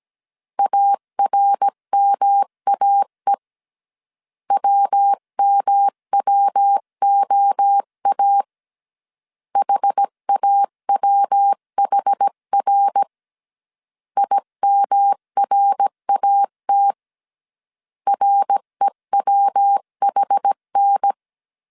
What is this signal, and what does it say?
Morse code, heard as a keyed tone, about 17 wpm, 800 Hz. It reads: ARMAE WMWOA HAWHR IMRAT REWHN
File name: CW / DET / 1 3 1500C S A 16wpm 2141